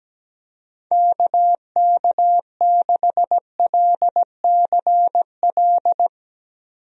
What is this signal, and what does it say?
Morse code, heard as a keyed tone, about 17 wpm, 700 Hz. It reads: KK6LCL
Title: USA Call Signs - 6 Characters - Repeated 2x - 17wpm